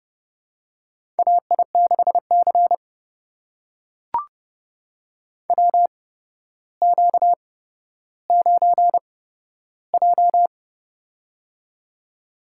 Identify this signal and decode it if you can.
Morse code, heard as a keyed tone, about 30 wpm, spaced as for 13 wpm, 700 Hz. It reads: AI6C WQ9J
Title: USA Call Signs - 4 Characters - 8x Character Spacing - 30wpm